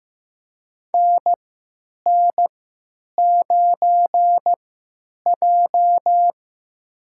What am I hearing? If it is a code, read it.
Morse code, heard as a keyed tone, about 15 wpm, spaced as for 8 wpm, 700 Hz. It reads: NN9J
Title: USA Call Signs - 3x Character Spacing - 15wpm